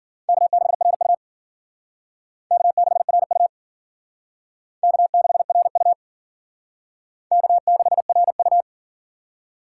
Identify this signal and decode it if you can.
Morse code, heard as a keyed tone, about 50 wpm, 700 Hz. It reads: K6RU K6RU K6RU K6RU
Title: Speed Racing - USA 1x2 Call Signs 40wpm